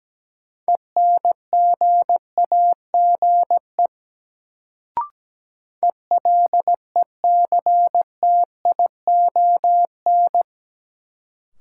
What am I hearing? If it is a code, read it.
Morse code, heard as a keyed tone, about 17 wpm, 700 Hz. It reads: ENGAGE ELECTION
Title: Sentences from Top 1500 Words - New Word Drill Alt - 17wpm